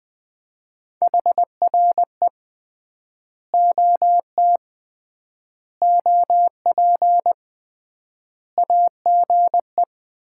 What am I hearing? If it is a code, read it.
Morse code, heard as a keyed tone, about 21 wpm, 700 Hz. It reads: HR E OT OP AGE
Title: Common QSO Phrases - 3x Word-Spacing - 20wpm